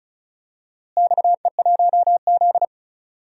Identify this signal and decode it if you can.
Morse code, heard as a keyed tone, about 36 wpm, 700 Hz. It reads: XE1Z
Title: DXCC-100 Call Signs - 4 Characters - Repeated 2x - 35wpm